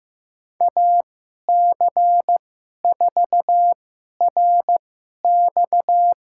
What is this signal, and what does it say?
Morse code, heard as a keyed tone, about 16 wpm, spaced as for 11 wpm, 700 Hz. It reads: AC4RX
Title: USA Call Signs - 5 Characters - 2x Character Spacing - 15wpm